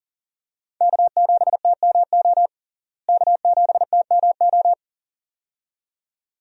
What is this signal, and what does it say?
Morse code, heard as a keyed tone, about 37 wpm, 700 Hz. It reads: K7TMO K7TMO
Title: USA Call Signs - 5 Characters - Repeated 2x - 40wpm